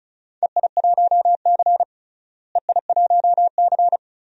Morse code, heard as a keyed tone, about 34 wpm, 700 Hz. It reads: EI1C EI1C
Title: DXCC-100 Call Signs - 4 Characters - Repeated 2x - 35wpm